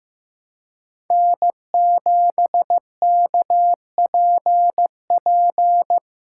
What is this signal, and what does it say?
Morse code, heard as a keyed tone, about 15 wpm, 700 Hz. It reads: N7KPP
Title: USA Call Signs - 4x Character Spacing - 15wpm